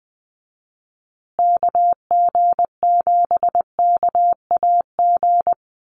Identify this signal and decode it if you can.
Morse code, heard as a keyed tone, about 20 wpm, 700 Hz. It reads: KG7KAG
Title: USA Call Signs Repeated 3x 20wpm